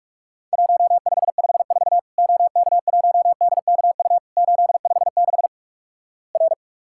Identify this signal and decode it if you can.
Morse code, heard as a keyed tone, about 46 wpm, 700 Hz. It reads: 1554 OK1DKU 856 R
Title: Worked All Europe DX Contest - QTC Exchange - 45wpm